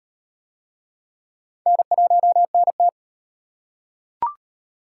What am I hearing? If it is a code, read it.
Morse code, heard as a keyed tone, about 38 wpm, 700 Hz. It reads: N1NT